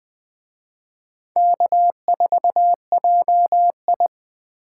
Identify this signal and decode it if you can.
Morse code, heard as a keyed tone, about 24 wpm, 700 Hz. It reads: K4JI
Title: Speed Racing - USA 1x2 Call Signs 20wpm